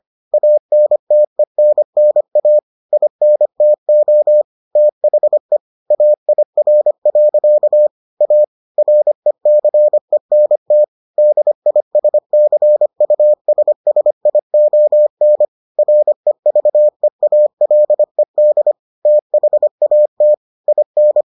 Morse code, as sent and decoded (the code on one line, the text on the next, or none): .- -. - . -. -. .- / .. -. - --- / - .... / .- .. .-. .-.-.- / .- / .-. -.-. -. - / -.. .. ... -.-. ..- ... ... .. --- -. / .-. ...- . .- .-.. -.. / - .... .- - / .. -.
ANTENNA INTO THE AIR. A RECENT DISCUSSION REVEALED THAT IN